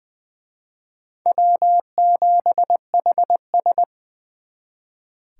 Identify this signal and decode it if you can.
Morse code, heard as a keyed tone, about 24 wpm, 700 Hz. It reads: W7HS
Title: Speed Racing - USA 1x2 Call Signs 20wpm